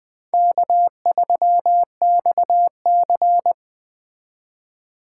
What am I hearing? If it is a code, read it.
Morse code, heard as a keyed tone, about 21 wpm, 700 Hz. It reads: K3XC